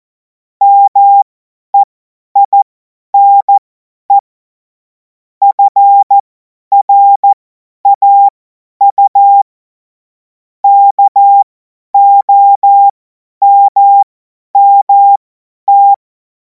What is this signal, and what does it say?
Morse code, heard as a keyed tone, about 14 wpm, spaced as for 10 wpm, 800 Hz. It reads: MEINE FRAU KOMMT
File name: Morsekurs / Witze / Witz 14. April 2026 50 F70